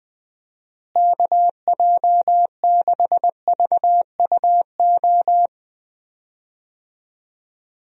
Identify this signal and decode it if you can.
Morse code, heard as a keyed tone, about 21 wpm, 700 Hz. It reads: KJ6VUO
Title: USA Call Signs - 6 Characters - Repeated 3x - 20wpm